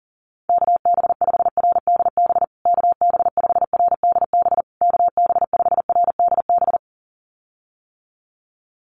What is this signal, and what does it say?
Morse code, heard as a keyed tone, about 40 wpm, 700 Hz. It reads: KB5RDB KB5RDB KB5RDB
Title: USA Call Signs Repeated 3x 40wpm